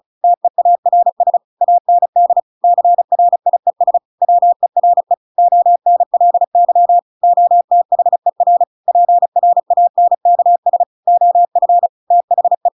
Morse code, heard as a keyed tone, about 34 wpm, 700 Hz. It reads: TEARS AND CRIES WERE ONLY OTHER PRANKS OF THE
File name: Pinocchio / Pinocchio ch7 35 35 ik1hge